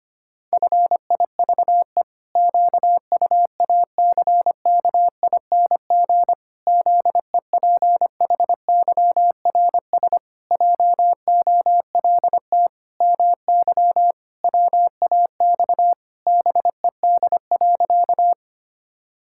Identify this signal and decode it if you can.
Morse code, heard as a keyed tone, about 25 wpm, 700 Hz. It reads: FIVE QUACKING ZEPHYRS JOLT MY WAX BED.
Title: Pangram Challenge v2 - 25wpm